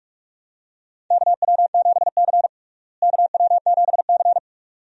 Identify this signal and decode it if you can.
Morse code, heard as a keyed tone, about 46 wpm, 700 Hz. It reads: KW7C KW7C
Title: USA Call Signs - 4 Characters - Repeated 2x - 45wpm